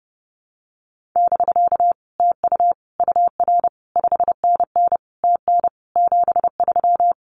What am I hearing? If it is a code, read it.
Morse code, heard as a keyed tone, about 30 wpm, 700 Hz. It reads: <BK> TU UR 5NN TN 73
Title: Common QSO Phrases Encore 30wpm